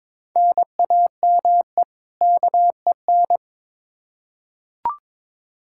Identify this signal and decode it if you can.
Morse code, heard as a keyed tone, about 22 wpm, 700 Hz. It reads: NAME KEN
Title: K1USN SST QSO Practice 22wpm